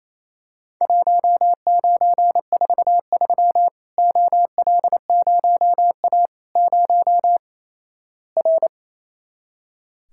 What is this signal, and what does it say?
Morse code, heard as a keyed tone, about 28 wpm, 700 Hz. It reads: 1943 OL0A 0 R
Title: Worked All Europe DX Contest - QTC Exchange - 28wpm